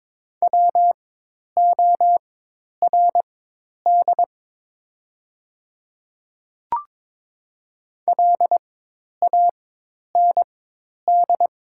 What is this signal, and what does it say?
Morse code, heard as a keyed tone, about 22 wpm, spaced as for 10 wpm, 700 Hz. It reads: WORD LAND
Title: 4x Character Spacing - Top 200 Words - 22wpm